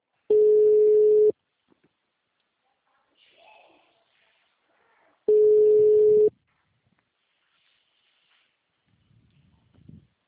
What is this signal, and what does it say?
Ringback tone (Europe, 425 Hz), with speech.